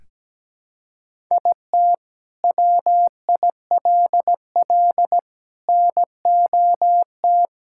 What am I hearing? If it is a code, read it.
Morse code, heard as a keyed tone, about 17 wpm, 700 Hz. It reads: IT WILL NOT
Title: Sentences from Top 1000 Words - Review Part 6 - 17wpm